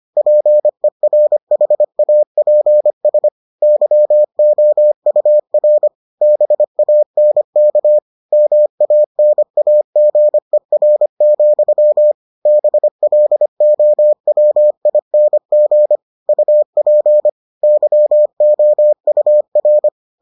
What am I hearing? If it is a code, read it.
Morse code, heard as a keyed tone, about 25 wpm, 600 Hz. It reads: PERHAPS YOUR BANK MANAGER, BLOWING UP YOUR